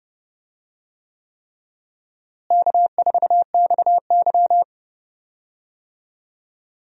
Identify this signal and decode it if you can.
Morse code, heard as a keyed tone, about 30 wpm, 700 Hz. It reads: K4XY
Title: USA Call Signs - 4 Characters - Repeated 3x - 30wpm